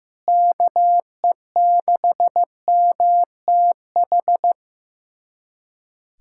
Morse code, heard as a keyed tone, about 15 wpm, 700 Hz. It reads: KE6MTH
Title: USA Callsign DE Callsign - 6 Characters - 15wpm